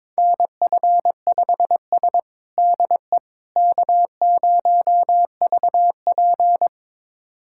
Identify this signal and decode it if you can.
Morse code, heard as a keyed tone, about 22 wpm, 700 Hz. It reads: NF5S DE K0VP
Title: USA Callsign DE Callsign - 4 Characters - 22wpm